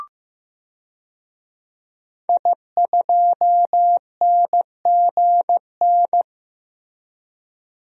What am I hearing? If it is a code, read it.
Morse code, heard as a keyed tone, about 15 wpm, 700 Hz. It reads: I2NGN
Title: DXCC-100 Call Signs - 5 Characters - Repeated 2x - 15wpm